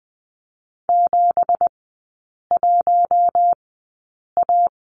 Morse code, heard as a keyed tone, about 20 wpm, spaced as for 8 wpm, 700 Hz. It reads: 71A
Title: 178 Lesson 29e - Slash - Three Characters 20wpm